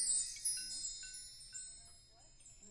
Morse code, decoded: none